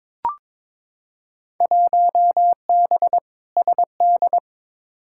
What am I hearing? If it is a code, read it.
Morse code, heard as a keyed tone, about 22 wpm, 700 Hz. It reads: 1B SD
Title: Field Day Exchanges Encore - 22wpm